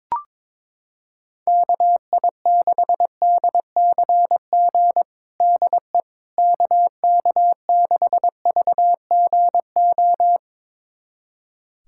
Morse code, heard as a keyed tone, about 22 wpm, 700 Hz. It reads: KI6DCG DE KK6VGO
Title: USA Callsign DE Callsign - 6 Characters - 22wpm